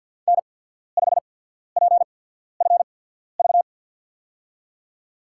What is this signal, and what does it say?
Morse code, heard as a keyed tone, about 53 wpm, spaced as for 14 wpm, 700 Hz. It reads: N5PFV
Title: USA Call Signs - 5 Characters - 8x Character Spacing - 50wpm